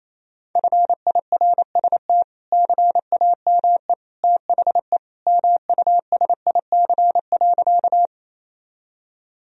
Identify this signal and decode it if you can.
Morse code, heard as a keyed tone, about 28 wpm, 700 Hz. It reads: FIRST CAME THE MUSIC.
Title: Sentences from Top 1000 Words - Review Part 1 - 28wpm